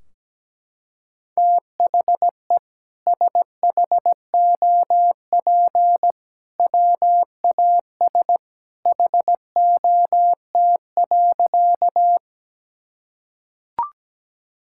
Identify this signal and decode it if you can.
Morse code, heard as a keyed tone, about 17 wpm, 700 Hz. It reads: THE SHOP WAS HOT.
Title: Sentences from Top 1600 Words 17wpm